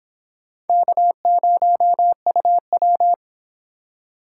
Morse code, heard as a keyed tone, about 29 wpm, 700 Hz. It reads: K0UW